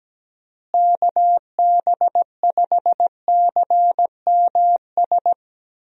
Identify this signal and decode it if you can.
Morse code, heard as a keyed tone, about 17 wpm, 700 Hz. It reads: KB5CMS
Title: USA Call Signs - 6 Characters - Repeated 3x - 17wpm